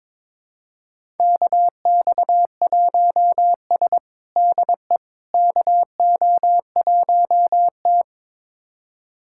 Morse code, heard as a keyed tone, about 22 wpm, 700 Hz. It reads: KX1S DE KO1T
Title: USA Callsign DE Callsign - 4 Characters - 22wpm